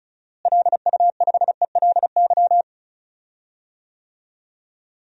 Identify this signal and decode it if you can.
Morse code, heard as a keyed tone, about 34 wpm, 700 Hz. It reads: LU5ELY
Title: DXCC-100 Call Signs - 6 Characters - Repeated 2x - 35wpm